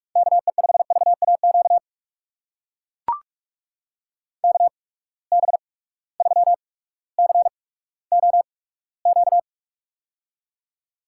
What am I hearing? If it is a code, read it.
Morse code, heard as a keyed tone, about 44 wpm, spaced as for 19 wpm, 700 Hz. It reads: KE5VAQ KB3COQ
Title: USA Call Signs - 6 Characters - 8x Character Spacing - 45wpm